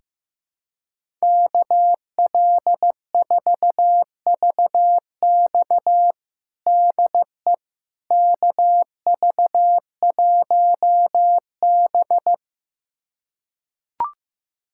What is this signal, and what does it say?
Morse code, heard as a keyed tone, about 15 wpm, 700 Hz. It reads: KL4VX DE KV1B
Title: USA Callsign DE Callsign 15wpm